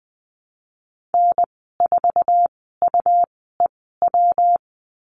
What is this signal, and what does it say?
Morse code, heard as a keyed tone, about 20 wpm, spaced as for 15 wpm, 700 Hz. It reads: N4UEW
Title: 140 Lesson 23g - Letter f - Call Signs 20wpm